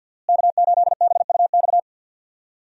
Morse code, heard as a keyed tone, about 46 wpm, 700 Hz. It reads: K8BUX
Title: USA Call Signs - 5 Characters - Repeated 3x - 50wpm